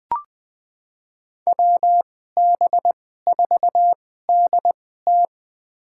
Morse code, heard as a keyed tone, about 21 wpm, spaced as for 15 wpm, 700 Hz. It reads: WB4DT
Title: USA Call Signs - 5 Characters - 2x Character Spacing - 20wpm